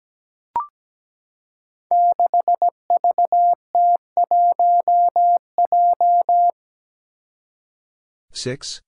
Morse code, heard as a keyed tone, about 17 wpm, 700 Hz. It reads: 6VT1J
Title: Five Challenging Letter-Number Combinations 17wpm